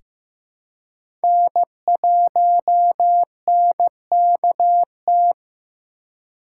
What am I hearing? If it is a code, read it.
Morse code, heard as a keyed tone, about 16 wpm, 700 Hz. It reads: N1NKT
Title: USA Call Signs - 5 Characters - 4x Character Spacing - 15wpm